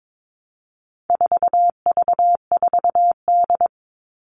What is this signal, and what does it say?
Morse code, heard as a keyed tone, about 22 wpm, 700 Hz. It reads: 4V4D
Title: All Cut Numbers - 4-Digit 22wpm